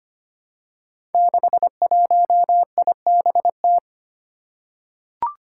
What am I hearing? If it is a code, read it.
Morse code, heard as a keyed tone, about 25 wpm, 700 Hz. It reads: 61IBT
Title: Five Challenging Letter-Number Combinations 25wpm